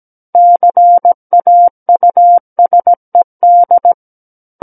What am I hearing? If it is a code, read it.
Morse code, heard as a keyed tone, about 17 wpm, 700 Hz. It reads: CAUSED
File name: Book One / 08-Chapter 8 - 17 WPM